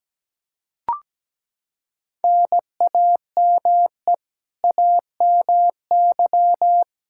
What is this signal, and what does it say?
Morse code, heard as a keyed tone, about 17 wpm, 700 Hz. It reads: NAME AMY
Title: Practice QSO - Exchange 1 - Beginning - 17wpm